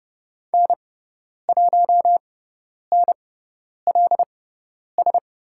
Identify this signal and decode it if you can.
Morse code, heard as a keyed tone, about 30 wpm, spaced as for 10 wpm, 700 Hz. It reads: N1NLS
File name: Lesson 16g - Letter c - Call Signs 30wpm